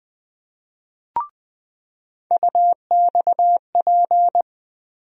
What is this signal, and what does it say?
Morse code, heard as a keyed tone, about 21 wpm, 700 Hz. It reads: UXP
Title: USA Call Sign Postfixes - 3 Letters 20wpm